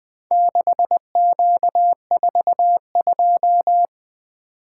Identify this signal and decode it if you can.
Morse code, heard as a keyed tone, about 21 wpm, 700 Hz. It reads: 6Q42
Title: Four Letter-Number Combinations - 20wpm